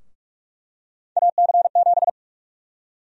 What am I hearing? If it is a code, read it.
Morse code, heard as a keyed tone, about 46 wpm, 700 Hz. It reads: AC7
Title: USA Call Sign Prefixes - 45wpm